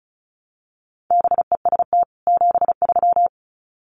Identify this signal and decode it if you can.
Morse code, heard as a keyed tone, about 34 wpm, 700 Hz. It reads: BEST 73
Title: Common QSO Phrases 35wpm